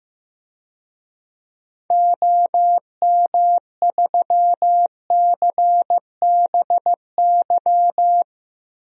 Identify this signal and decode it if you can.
Morse code, heard as a keyed tone, about 15 wpm, 700 Hz. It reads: OM3CBY
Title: DXCC-100 Call Signs - 6 Characters - 15wpm